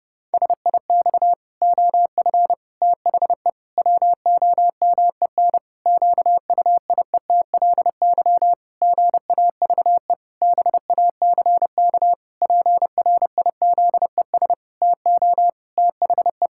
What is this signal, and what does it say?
Morse code, heard as a keyed tone, about 30 wpm, 700 Hz. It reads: SIX OF THE WOMEN QUIETLY GAVE BACK PRIZES TO THE
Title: Pangram Challenge v2 - 30wpm